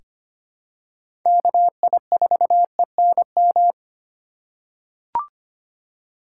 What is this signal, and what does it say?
Morse code, heard as a keyed tone, about 25 wpm, 700 Hz. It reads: KI4ENM
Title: USA Call Signs - 6x Character Spacing - 25wpm